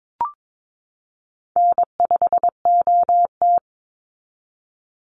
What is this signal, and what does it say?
Morse code, heard as a keyed tone, about 22 wpm, 700 Hz. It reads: N5OT
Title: Easy Call Signs Repeated 2x - 22wpm